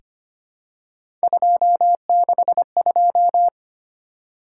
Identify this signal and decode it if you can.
Morse code, heard as a keyed tone, about 25 wpm, 700 Hz. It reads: 262
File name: Three Letter-Number Combinations - 25wpm